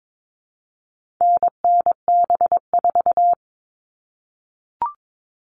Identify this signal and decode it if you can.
Morse code, heard as a keyed tone, about 22 wpm, 700 Hz. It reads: NNB4
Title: All Cut Numbers - 4-Digit 22wpm